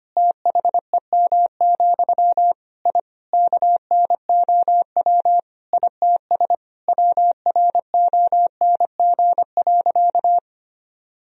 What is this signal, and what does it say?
Morse code, heard as a keyed tone, about 25 wpm, 700 Hz. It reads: THEM, I KNOW ITS WRONG.
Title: Sentences from Top 1000 Words - Review Part 1 - 25wpm